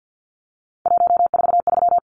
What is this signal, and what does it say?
Morse code, heard as a keyed tone, about 50 wpm, 700 Hz. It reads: 143
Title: Numbers - 3-Digit 50wpm